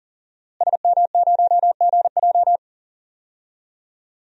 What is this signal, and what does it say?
Morse code, heard as a keyed tone, about 40 wpm, 700 Hz. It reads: SM0GJ
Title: DXCC-100 Call Signs - 5 Characters - Repeated 3x - 40wpm